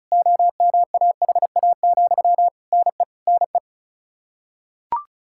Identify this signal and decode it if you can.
Morse code, heard as a keyed tone, about 34 wpm, 700 Hz. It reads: OMAHA, NE NE
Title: Practice QSO - Exchange 1 - Beginning - 35wpm